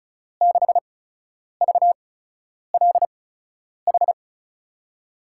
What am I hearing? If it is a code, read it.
Morse code, heard as a keyed tone, about 36 wpm, spaced as for 10 wpm, 700 Hz. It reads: 6VLH
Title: USA Call Signs - 6 Characters - 8x Character Spacing - 35wpm